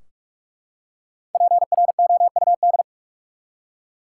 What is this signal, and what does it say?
Morse code, heard as a keyed tone, about 43 wpm, 700 Hz. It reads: PROUD
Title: Five Letter Words - 45wpm